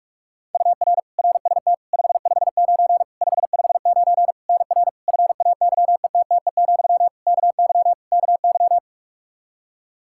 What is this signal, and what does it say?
Morse code, heard as a keyed tone, about 42 wpm, 700 Hz. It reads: UR RST 559 559 NR FAYETTE, KY KY
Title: Practice QSO - Exchange 1 - Beginning - 45wpm